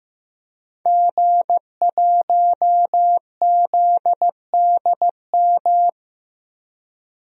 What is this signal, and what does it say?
Morse code, heard as a keyed tone, about 15 wpm, 700 Hz. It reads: G1ZDM
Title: DXCC-100 Call Signs - 5 Characters - 15wpm